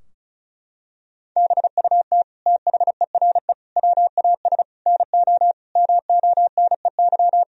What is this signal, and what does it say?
Morse code, heard as a keyed tone, about 34 wpm, 700 Hz. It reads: BUT THERE WAS NO MONEY
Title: Sentences from Top 1000 Words - Review Part 2 - 35wpm